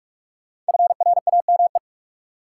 Morse code, heard as a keyed tone, about 46 wpm, 700 Hz. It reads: FRAME
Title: Sentences from Top 1400 Words - New Word Drill Alt - 45wpm